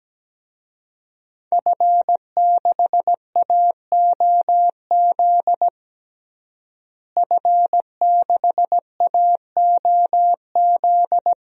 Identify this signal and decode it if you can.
Morse code, heard as a keyed tone, about 17 wpm, 700 Hz. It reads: F6AOZ F6AOZ
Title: DXCC-100 Call Signs - 5 Characters - Repeated 3x - 17wpm